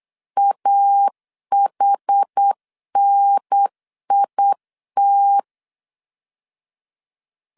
Morse code, heard as a keyed tone, about 8 wpm, 800 Hz. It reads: AHNIT
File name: CW / DET / 1 3 1500C S B 8wpm 4141